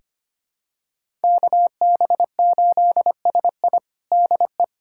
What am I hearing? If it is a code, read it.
Morse code, heard as a keyed tone, about 25 wpm, 700 Hz. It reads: KB8SI DE
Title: USA Callsign DE Callsign 25wpm